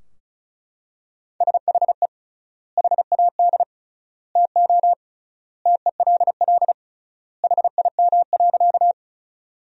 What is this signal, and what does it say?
Morse code, heard as a keyed tone, about 36 wpm, 700 Hz. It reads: SHE HAD TO TELL HIM.
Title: 3x Word-Spacing - Sentences from Top 200 Words - 35wpm